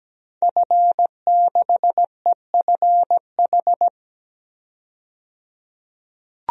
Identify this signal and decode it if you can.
Morse code, heard as a keyed tone, about 17 wpm, 700 Hz. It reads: F6EFH
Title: DXCC-100 Call Signs - 5 Characters - 17wpm